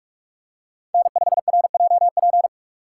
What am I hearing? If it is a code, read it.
Morse code, heard as a keyed tone, about 40 wpm, 700 Hz. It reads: N5RJP